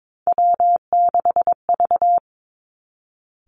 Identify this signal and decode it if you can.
Morse code, heard as a keyed tone, about 22 wpm, 700 Hz. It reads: W6V
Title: Three Challenging Letter-Number Combinations 22wpm